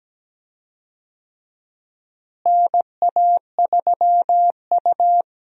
Morse code, heard as a keyed tone, about 17 wpm, 700 Hz. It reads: NA3U
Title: USA Call Signs - 4 Characters - Repeated 3x - 17wpm